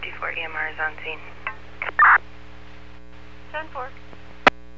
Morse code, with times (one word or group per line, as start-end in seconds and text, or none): none